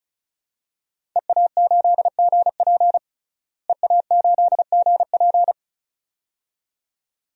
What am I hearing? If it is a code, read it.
Morse code, heard as a keyed tone, about 34 wpm, 700 Hz. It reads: EA8GP EA8GP
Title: DXCC-100 Call Signs - 5 Characters - Repeated 2x - 35wpm